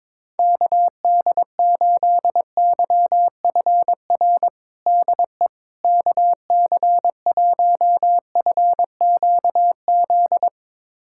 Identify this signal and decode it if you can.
Morse code, heard as a keyed tone, about 22 wpm, 700 Hz. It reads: KD8YFR DE KC1FQZ
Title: USA Callsign DE Callsign 22wpm